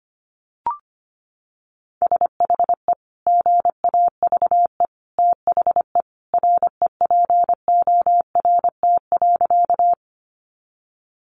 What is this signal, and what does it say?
Morse code, heard as a keyed tone, about 25 wpm, 700 Hz. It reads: SHE GAVE THE REPORT.
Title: Sentences from Top 900 Words 25wpm